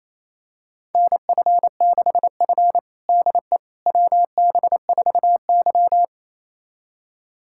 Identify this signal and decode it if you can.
Morse code, heard as a keyed tone, about 28 wpm, 700 Hz. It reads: NF6F DE WB4Y
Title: USA Callsign DE Callsign - 4 Characters - 28wpm